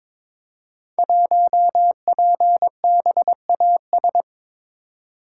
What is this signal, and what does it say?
Morse code, heard as a keyed tone, about 23 wpm, 700 Hz. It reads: 1PBAS
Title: Code Groups - 5 Letter:Numbers - 22wpm